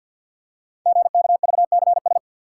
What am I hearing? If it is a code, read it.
Morse code, heard as a keyed tone, about 50 wpm, 690 Hz. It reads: GKVCS
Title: Code Groups - 5 Letters - 50wpm